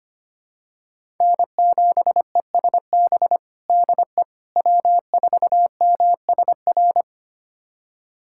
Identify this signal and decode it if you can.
Morse code, heard as a keyed tone, about 25 wpm, 700 Hz. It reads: N7ESB DE W4MSR
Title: USA Callsign DE Callsign - 5 Characters - 25wpm